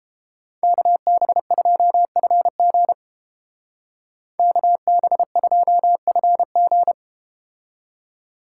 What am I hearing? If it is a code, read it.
Morse code, heard as a keyed tone, about 32 wpm, 700 Hz. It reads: KB2FG KB2FG